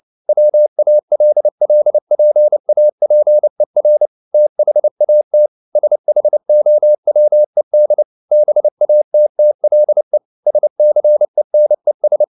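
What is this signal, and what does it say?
Morse code, heard as a keyed tone, about 29 wpm, 600 Hz. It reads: WALLPAPER THAT SHOWED BATTLE SCENES
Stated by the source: Morse code at 29 wpm, sending Ken Follett's novel Never in English, from CW med Gnister.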